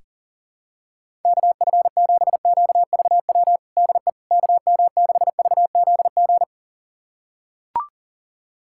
Morse code, heard as a keyed tone, about 40 wpm, 700 Hz. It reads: KF7QVW DE KM6VZG
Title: USA Callsign DE Callsign 40wpm